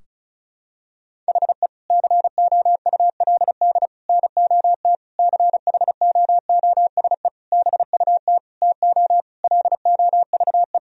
Morse code, heard as a keyed tone, about 34 wpm, 700 Hz. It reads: HE COULD NOT CHOOSE BUT TO LOVE
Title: Sentences from Top 1000 Words - Review Part 4 - 35wpm